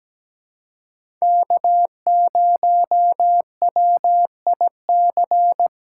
Morse code, heard as a keyed tone, about 17 wpm, 700 Hz. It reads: K0WIC
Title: USA Call Signs - 5 Characters - 2x Character Spacing - 17wpm